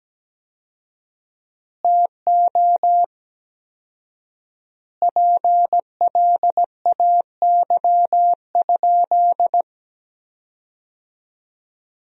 Morse code, heard as a keyed tone, about 17 wpm, 700 Hz. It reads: TO PLAY?
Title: 4x Word-Spacing - Sentences from Top 400 Words - 17wpm